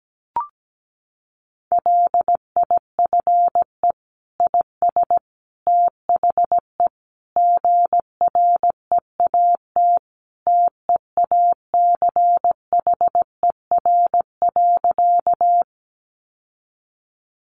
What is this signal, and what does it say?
Morse code, heard as a keyed tone, about 17 wpm, 700 Hz. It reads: LIFE IS THE GREAT TEACHER.
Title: Sentences from Top 900 Words 17wpm